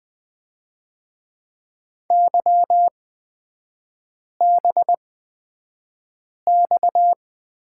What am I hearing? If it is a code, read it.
Morse code, heard as a keyed tone, about 20 wpm, spaced as for 5 wpm, 690 Hz. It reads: YBX
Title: 236 Lesson 38e - Letter q - Three Characters 20wpm